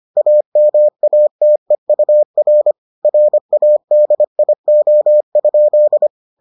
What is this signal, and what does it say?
Morse code, heard as a keyed tone, about 25 wpm, 600 Hz. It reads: AMATEUR RADIO?